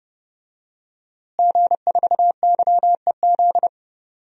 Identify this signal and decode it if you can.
Morse code, heard as a keyed tone, about 30 wpm, 700 Hz. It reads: G4YEZ
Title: DXCC-100 Call Signs - 5 Characters - Repeated 3x - 30wpm